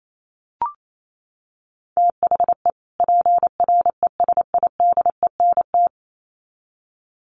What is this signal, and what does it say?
Morse code, heard as a keyed tone, about 28 wpm, 700 Hz. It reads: THE PRESIDENT
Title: Sets of 2 Words 28wpm